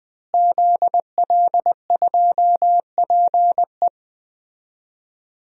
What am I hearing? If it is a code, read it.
Morse code, heard as a keyed tone, about 20 wpm, 700 Hz. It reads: ZL2PE
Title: DXCC-100 Call Signs - 5 Characters - Repeated 3x - 20wpm